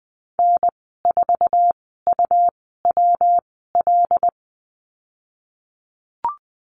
Morse code, heard as a keyed tone, about 20 wpm, spaced as for 15 wpm, 700 Hz. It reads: N4UWL
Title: 167 Lesson 27g - Number 7 - Call Signs 20wpm